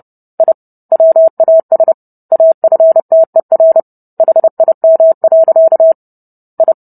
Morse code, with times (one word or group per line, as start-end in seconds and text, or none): 0.4-0.5 I
0.9-1.9 WAS
2.3-3.8 AFTER
4.2-5.9 HIM.
6.6-6.7 I